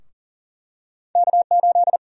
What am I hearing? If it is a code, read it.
Morse code, heard as a keyed tone, about 40 wpm, 700 Hz. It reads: K8
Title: USA Call Sign Prefixes - 40wpm